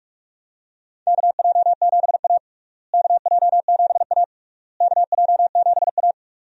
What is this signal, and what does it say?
Morse code, heard as a keyed tone, about 41 wpm, 700 Hz. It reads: KJ7A KJ7A KJ7A
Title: USA Call Signs - 4 Characters - Repeated 3x - 45wpm